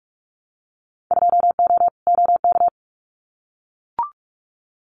Morse code, heard as a keyed tone, about 44 wpm, 700 Hz. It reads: 2O OK
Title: Winter Field Day Exchanges 45wpm